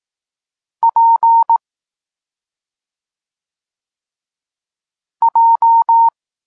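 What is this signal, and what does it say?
Morse code, heard as a keyed tone, about 18 wpm, 920 Hz. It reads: P J
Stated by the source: Polish Morse lesson teaching the letter C, from SP9MOA Niepołomicki Klub Krótkofalowców.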